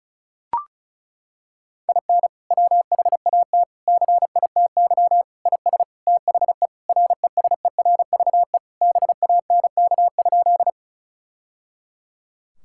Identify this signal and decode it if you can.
Morse code, heard as a keyed tone, about 36 wpm, 700 Hz. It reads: IN WHAT CITY IS THE RESERVE BANK?
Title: Sentences from Top 1000 Words - Review Part 6 - 35wpm